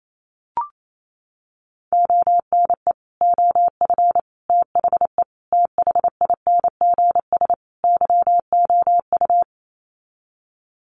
Morse code, heard as a keyed tone, about 28 wpm, 700 Hz. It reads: ONE OF THE THINGS YOU
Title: Sets of 5 Words Encore2 28wpm